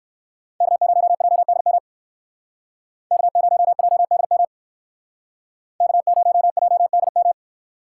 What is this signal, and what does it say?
Morse code, heard as a keyed tone, about 58 wpm, 700 Hz. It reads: K0JDM K0JDM K0JDM